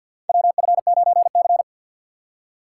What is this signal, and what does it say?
Morse code, heard as a keyed tone, about 47 wpm, 700 Hz. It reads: WF9C
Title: USA Call Signs - 4 Characters - 6x Character Spacing - 50wpm